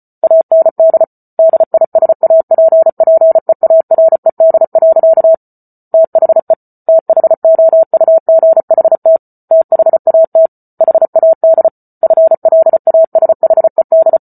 Morse code, heard as a keyed tone, about 34 wpm, 650 Hz. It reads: AND DISAPPEARED. THE THOUGHT THAT HAD FLASHED
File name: Book Two / 08-Chapter 8 - 34 WPM